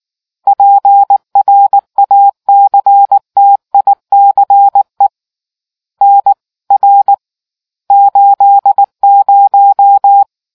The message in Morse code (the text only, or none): PRACTICE NR 80